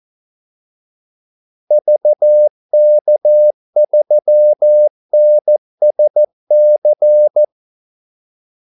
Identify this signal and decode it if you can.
Morse code, heard as a keyed tone, about 14 wpm, 600 Hz. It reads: VK3NSC
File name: Callsigns-14wpm